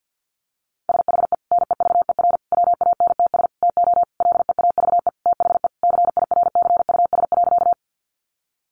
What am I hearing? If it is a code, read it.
Morse code, heard as a keyed tone, about 48 wpm, 700 Hz. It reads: SHE NEVER WANTS TO LEAVE THE CIRCUS.
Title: Easy Phrases 50wpm